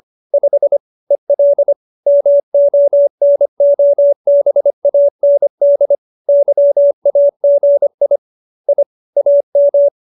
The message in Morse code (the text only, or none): 5 EL MONOBAND YAGI I AM